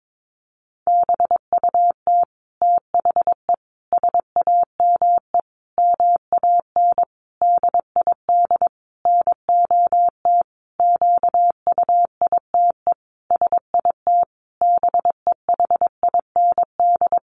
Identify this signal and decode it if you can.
Morse code, heard as a keyed tone, about 22 wpm, 700 Hz. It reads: BUT THE SAME MAN DID NOT QUITE SIT BEHIND
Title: A Sound Of Thunder 22wpm